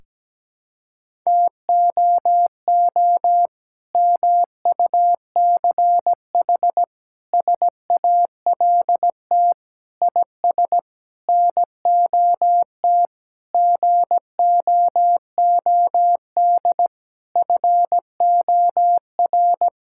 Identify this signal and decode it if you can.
Morse code, heard as a keyed tone, about 17 wpm, 700 Hz. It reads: TOO MUCH SALT IS NOT GOOD FOR
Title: Sentences from Top 1000 Words - Review Part 5 - 17wpm